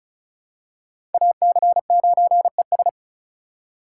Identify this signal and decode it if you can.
Morse code, heard as a keyed tone, about 36 wpm, 700 Hz. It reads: AC9ES
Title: USA Call Signs - 5 Characters - 4x Character Spacing - 35wpm